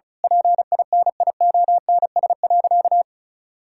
Morse code, heard as a keyed tone, about 36 wpm, 700 Hz. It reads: PINIONS.
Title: Pangram Challenge v2 - 35wpm